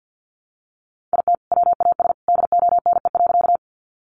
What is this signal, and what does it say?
Morse code, heard as a keyed tone, about 50 wpm, 700 Hz. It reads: IT WAS DONE.
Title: Sentences from Top 500 Words 50wpm